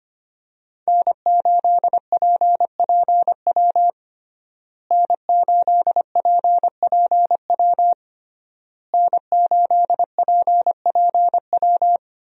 Morse code, heard as a keyed tone, about 25 wpm, 700 Hz. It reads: N8PPW N8PPW N8PPW
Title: USA Call Signs - 5 Characters - Repeated 3x - 25wpm